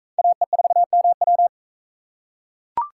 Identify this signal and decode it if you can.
Morse code, heard as a keyed tone, about 42 wpm, 700 Hz. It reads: AE4MW